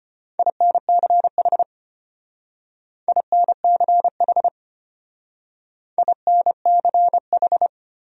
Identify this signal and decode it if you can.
Morse code, heard as a keyed tone, about 31 wpm, 700 Hz. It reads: INCH INCH INCH
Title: Speed-Racing Top 500 Words 25wpm